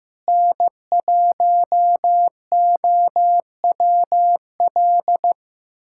Morse code, heard as a keyed tone, about 15 wpm, 700 Hz. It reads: N1OWL
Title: USA Callsign DE Callsign - 5 Characters - 15wpm